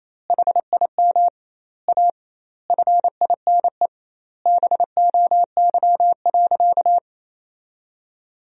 Morse code, heard as a keyed tone, about 28 wpm, 700 Hz. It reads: HIM A FINE BOY.
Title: 2x Word-Spacing - Sentences from Top 500 Words - 28wpm